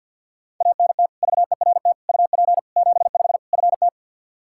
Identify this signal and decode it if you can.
Morse code, heard as a keyed tone, about 52 wpm, 700 Hz. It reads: ANT VERT UP 75 FT
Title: Practice QSO - Exchange 2 - Info - 50wpm